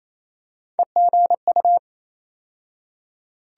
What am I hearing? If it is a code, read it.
Morse code, heard as a keyed tone, about 28 wpm, 700 Hz. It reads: EGU
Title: USA Call Sign Postfixes - 3 Letters 28wpm